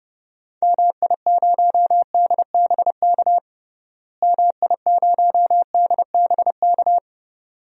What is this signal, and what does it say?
Morse code, heard as a keyed tone, about 30 wpm, 700 Hz. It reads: MI0DBK MI0DBK
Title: DXCC-100 Call Signs - 6 Characters - Repeated 2x - 30wpm